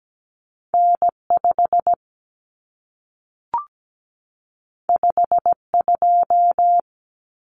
Morse code, heard as a keyed tone, about 17 wpm, 700 Hz. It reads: N5 52
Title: Common Cut Numbers - 2-Digit 17wpm